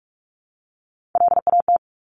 Morse code, heard as a keyed tone, about 46 wpm, 700 Hz. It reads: LAT
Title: Top Three Letter Combinations in Words 45wpm